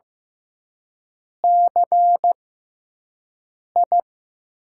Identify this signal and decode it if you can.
Morse code, heard as a keyed tone, about 15 wpm, spaced as for 5 wpm, 700 Hz. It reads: CI
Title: USA Call Signs - 6x Character Spacing - 15wpm